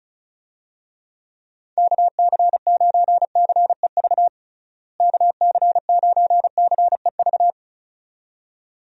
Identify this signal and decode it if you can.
Morse code, heard as a keyed tone, about 36 wpm, 700 Hz. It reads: KC9CEV KC9CEV
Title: USA Call Signs - 6 Characters - Repeated 2x - 35wpm